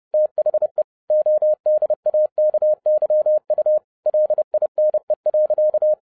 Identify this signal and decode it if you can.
Morse code, heard as a keyed tone, about 30 wpm, 610 Hz. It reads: THE ODAKYU LINE.